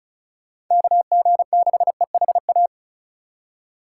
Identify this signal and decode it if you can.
Morse code, heard as a keyed tone, about 36 wpm, 700 Hz. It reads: KG6EHA
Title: USA Call Signs - 6 Characters - 2x Character Spacing - 35wpm